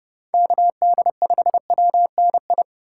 Morse code, heard as a keyed tone, about 30 wpm, 700 Hz. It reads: KD5WNI